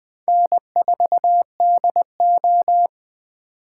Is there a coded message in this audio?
Yes: Morse code from 0.3 to 2.9 s, heard as a keyed tone, about 19 wpm, 700 Hz. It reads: N4DO